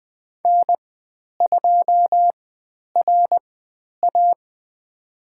Morse code, heard as a keyed tone, about 20 wpm, spaced as for 10 wpm, 700 Hz. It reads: N2RA
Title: 095 Lesson 16g - Letter c - Call Signs 20wpm